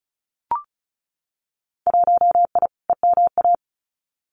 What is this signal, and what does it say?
Morse code, heard as a keyed tone, about 34 wpm, 700 Hz. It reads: 1I EMA
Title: Winter Field Day Exchanges Encore 35wpm